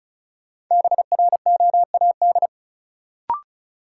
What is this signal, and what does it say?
Morse code, heard as a keyed tone, about 36 wpm, 700 Hz. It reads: BROAD
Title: Sentences from Top 1000 Words - New Word Drill - 35wpm